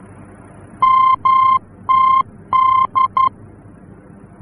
Morse code, heard as a keyed tone, about 11 wpm, 1050 Hz. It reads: MTD
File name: france / mx / MTD-F-VOR